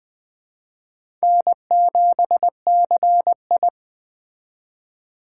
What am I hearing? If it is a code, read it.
Morse code, heard as a keyed tone, about 21 wpm, 700 Hz. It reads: N7CI